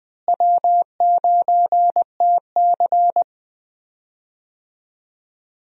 Morse code, heard as a keyed tone, about 20 wpm, 700 Hz. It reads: W9TC
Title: USA Call Signs - 4 Characters - Repeated 2x - 20wpm